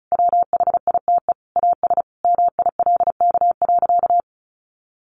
Morse code, heard as a keyed tone, about 34 wpm, 700 Hz. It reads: WHITE AS MILK.
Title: Sentences from Top 1500 Words 35wpm